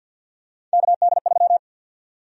Morse code, heard as a keyed tone, about 45 wpm, 700 Hz. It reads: KD3
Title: USA Call Sign Prefixes - 50wpm